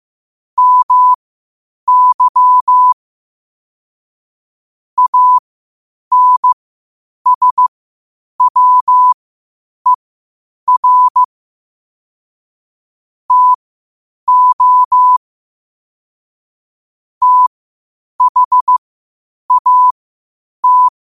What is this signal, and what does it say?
Morse code, heard as a keyed tone, about 15 wpm, spaced as for 8 wpm, 1000 Hz. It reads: MY ANSWER TO THAT